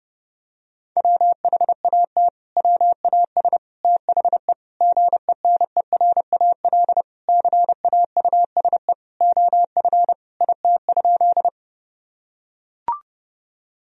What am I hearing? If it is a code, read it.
Morse code, heard as a keyed tone, about 30 wpm, 700 Hz. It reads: WHAT WAS THE GENERAL CAUSE OF IT?
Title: Sentences from Top 1000 Words - Review Part 3 - 30wpm